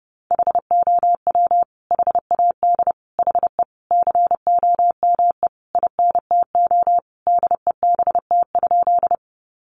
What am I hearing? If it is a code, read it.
Morse code, heard as a keyed tone, about 30 wpm, 700 Hz. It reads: HOW HAD HE COME INTO DEBT?
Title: Sentences from Top 1300 Words 30wpm